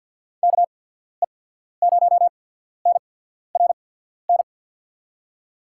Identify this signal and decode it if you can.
Morse code, heard as a keyed tone, about 54 wpm, spaced as for 14 wpm, 700 Hz. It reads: KE0NRN
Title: USA Call Signs - 6 Characters - 8x Character Spacing - 50wpm